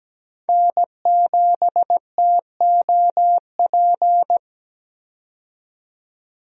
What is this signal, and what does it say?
Morse code, heard as a keyed tone, about 17 wpm, 700 Hz. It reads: N7TOP
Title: USA Call Signs - 5 Characters - 17wpm